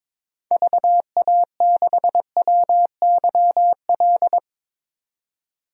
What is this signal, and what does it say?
Morse code, heard as a keyed tone, about 22 wpm, 700 Hz. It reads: VA6WYL
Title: DXCC-100 Call Signs - 6 Characters - 22wpm